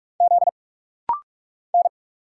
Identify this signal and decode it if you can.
Morse code, heard as a keyed tone, about 45 wpm, 700 Hz. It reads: Z N
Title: Single Letter-Number - Rapid-Fire 45wpm